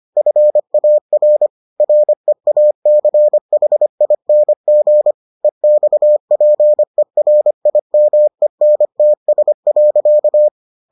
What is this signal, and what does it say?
Morse code, heard as a keyed tone, about 25 wpm, 600 Hz. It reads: FAR REACHING EXPERIMENTS.